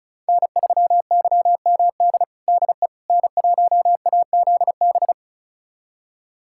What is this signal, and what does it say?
Morse code, heard as a keyed tone, about 36 wpm, 690 Hz. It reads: N3YMD DE N1AZB
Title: USA Callsign DE Callsign - 5 Characters - 35wpm